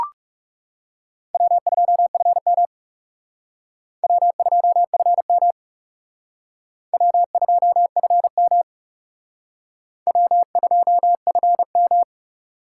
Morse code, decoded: W2FM W2FM W2FM W2FM

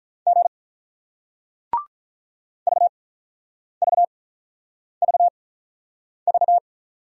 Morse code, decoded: G VVVV

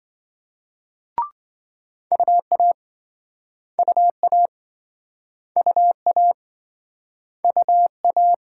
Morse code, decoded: UA UA UA UA

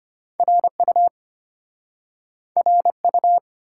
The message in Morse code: RU RU

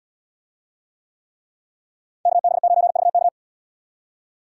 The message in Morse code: KB9VK

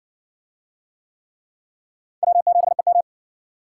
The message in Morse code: W7EM

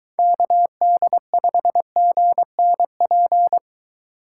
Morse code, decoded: KD5GNP

